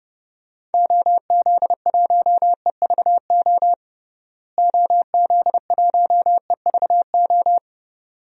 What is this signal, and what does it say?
Morse code, heard as a keyed tone, about 30 wpm, 700 Hz. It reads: OZ1EVO OZ1EVO
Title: DXCC-100 Call Signs - 6 Characters - Repeated 2x - 30wpm